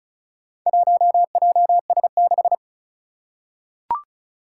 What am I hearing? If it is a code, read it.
Morse code, heard as a keyed tone, about 36 wpm, 700 Hz. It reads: 1JS6
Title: Four Challenging Letter-Number Combinations 35wpm